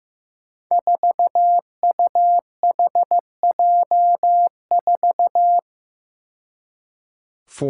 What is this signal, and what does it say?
Morse code, heard as a keyed tone, about 15 wpm, 700 Hz. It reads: 4UHJ4
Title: Five Challenging Letter-Number Combinations 15wpm